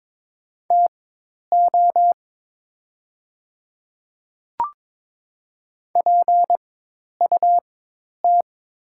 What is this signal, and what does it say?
Morse code, heard as a keyed tone, about 22 wpm, spaced as for 10 wpm, 700 Hz. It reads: TO PUT
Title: 4x Character Spacing - Top 200 Words - 22wpm